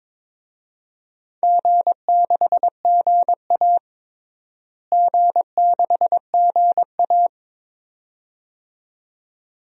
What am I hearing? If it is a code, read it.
Morse code, heard as a keyed tone, about 22 wpm, 700 Hz. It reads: G6GA G6GA
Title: DXCC-100 Call Signs - 4 Characters - Repeated 2x - 22wpm